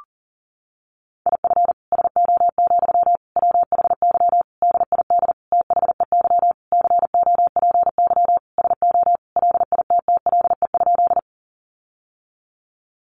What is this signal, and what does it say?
Morse code, heard as a keyed tone, about 40 wpm, 700 Hz. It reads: IF SO, WHY DID THEY COPY SO LITTLE?
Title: Sentences from Top 1400 Words 40wpm